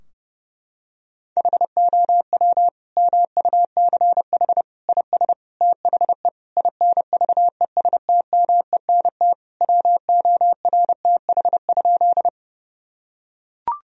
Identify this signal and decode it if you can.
Morse code, heard as a keyed tone, about 30 wpm, 700 Hz. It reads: HOW MUCH IS THE INVESTMENT WORTH?
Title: Sentences from Top 1000 Words - Review Part 4 - 30wpm